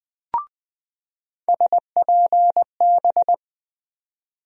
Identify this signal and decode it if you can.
Morse code, heard as a keyed tone, about 20 wpm, 700 Hz. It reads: SPB
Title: USA Call Sign Postfixes - 3 Letters 20wpm